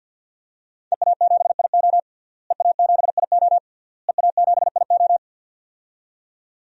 Morse code, heard as a keyed tone, about 50 wpm, 700 Hz. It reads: EA7IO EA7IO EA7IO
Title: DXCC-100 Call Signs - 5 Characters - Repeated 3x - 50wpm